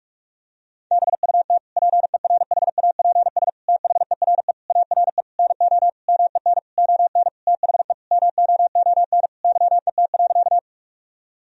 Morse code, heard as a keyed tone, about 46 wpm, 700 Hz. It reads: BUT PERHAPS THERE ARE NO MEN ON THE MOON YET.
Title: Sentences from Top 1000 Words - Review Part 3 - 45wpm